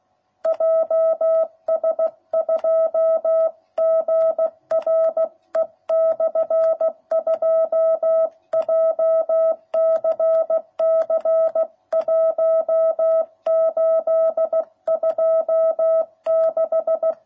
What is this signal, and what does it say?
Morse code, heard as a keyed tone, about 16 wpm, 660 Hz. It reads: JS2GRE/2 JCC 1826